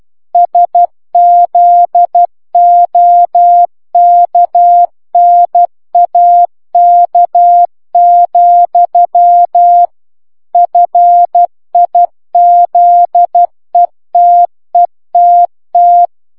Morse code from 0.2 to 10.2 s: SZOKNAK,